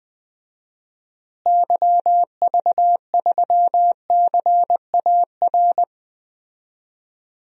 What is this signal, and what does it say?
Morse code, heard as a keyed tone, about 21 wpm, 700 Hz. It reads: YV3CAR
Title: DXCC-100 Call Signs - 6 Characters - Repeated 2x - 20wpm